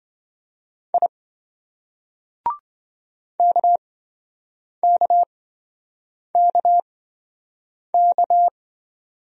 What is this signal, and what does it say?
Morse code, heard as a keyed tone, about 26 wpm, spaced as for 7 wpm, 700 Hz. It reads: I KKKK